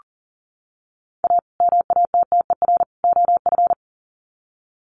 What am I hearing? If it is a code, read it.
Morse code, heard as a keyed tone, about 40 wpm, 700 Hz. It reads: A MATTER OF
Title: Sets of 3 Words 40wpm